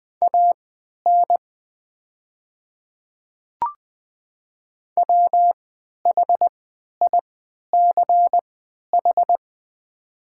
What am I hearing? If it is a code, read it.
Morse code, heard as a keyed tone, about 21 wpm, spaced as for 11 wpm, 700 Hz. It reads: AN WHICH
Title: 3x Character Spacing - Top 100 Words - 20wpm